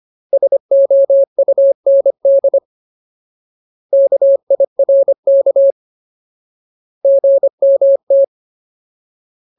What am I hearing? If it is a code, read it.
Morse code, heard as a keyed tone, about 25 wpm, 550 Hz. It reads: SOUND KIRK GMT
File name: Ultimate Word List / 25WPM Ultimate